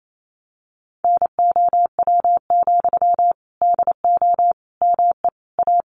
Morse code, heard as a keyed tone, about 28 wpm, 700 Hz. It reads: NOW, DO ME A
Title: Sentences from Top 700 Words 28wpm